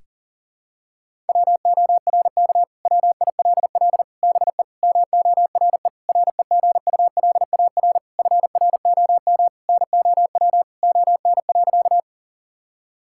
Morse code, heard as a keyed tone, about 40 wpm, 690 Hz. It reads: WORK WILL BE MORE REGULAR FROM NOW ON.
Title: Sentences from Top 1000 Words - Review Part 6 - 40wpm